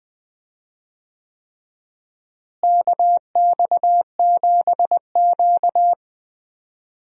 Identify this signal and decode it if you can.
Morse code, heard as a keyed tone, about 20 wpm, 700 Hz. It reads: KX7Q
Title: USA Call Signs - 4 Characters - Repeated 3x - 20wpm